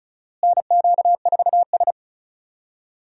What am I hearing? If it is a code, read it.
Morse code, heard as a keyed tone, about 34 wpm, 700 Hz. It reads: NQ4S
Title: USA Call Signs - 4 Characters - Repeated 2x - 35wpm